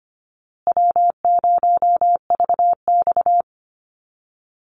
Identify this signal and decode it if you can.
Morse code, heard as a keyed tone, about 25 wpm, 700 Hz. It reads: W0VX
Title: Easy Call Signs Repeated 3x - 25wpm